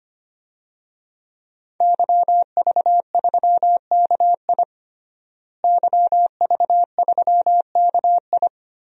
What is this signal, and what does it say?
Morse code, heard as a keyed tone, about 25 wpm, 700 Hz. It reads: YV3KI YV3KI
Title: DXCC-100 Call Signs - 5 Characters - Repeated 2x - 25wpm